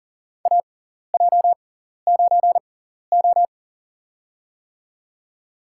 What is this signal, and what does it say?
Morse code, heard as a keyed tone, about 40 wpm, spaced as for 14 wpm, 700 Hz. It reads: AJ9O
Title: USA Call Signs - 4 Characters - 6x Character Spacing - 40wpm